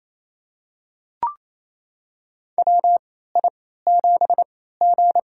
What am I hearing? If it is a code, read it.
Morse code, heard as a keyed tone, about 28 wpm, spaced as for 16 wpm, 700 Hz. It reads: WI7G
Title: USA Call Signs - 4 Characters - 3x Character Spacing - 28wpm